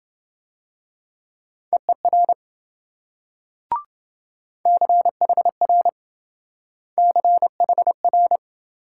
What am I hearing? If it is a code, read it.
Morse code, heard as a keyed tone, about 26 wpm, 700 Hz. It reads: EER CHR CHR